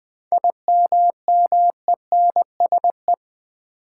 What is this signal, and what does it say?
Morse code, heard as a keyed tone, about 20 wpm, 700 Hz. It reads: IMMENSE
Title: 7 Letter Words - First 3 Letters Given - 20wpm